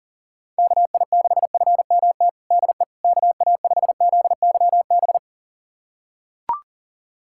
Morse code, heard as a keyed tone, about 40 wpm, 700 Hz. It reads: KI6FMT DE KA5ZYB
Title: USA Callsign DE Callsign - 6 Characters - 40wpm